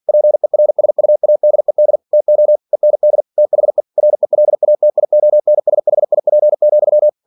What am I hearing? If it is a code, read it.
Morse code, heard as a keyed tone, about 47 wpm, 600 Hz. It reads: PERSUADED TO END THE RELATIONSHIP,
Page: Never 47wpm